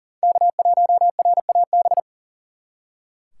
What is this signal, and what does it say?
Morse code, heard as a keyed tone, about 42 wpm, 700 Hz. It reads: K1RAB
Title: Speed Racing - USA 1x3 Call Signs 40wpm